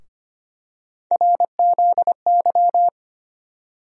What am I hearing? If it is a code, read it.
Morse code, heard as a keyed tone, about 25 wpm, 700 Hz. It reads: RZY
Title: USA Call Sign Postfixes - 3 Letters 25wpm